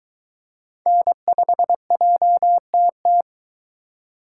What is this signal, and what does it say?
Morse code, heard as a keyed tone, about 22 wpm, 700 Hz. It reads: N5JTT